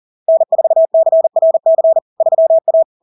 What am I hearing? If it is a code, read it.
Morse code, heard as a keyed tone, about 40 wpm, 650 Hz. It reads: N4CRC 3A
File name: LARK-FD-2022-Log-in-CW-40wpm-650-Hz